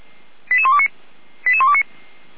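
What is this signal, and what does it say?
Five-tone selective call (ZVEI), heard as quick runs of five tones, tones of about 70 ms. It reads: 88129 88129
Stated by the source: ZVEI five-tone selective call 88235 received on an SDR, from the Feuerwehr Orlamünde audio server.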